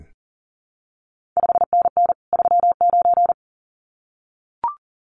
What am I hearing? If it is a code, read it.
Morse code, heard as a keyed tone, about 41 wpm, 700 Hz. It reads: 5NN 39
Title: CQ WW DX Contest Exchange Encore - 40wpm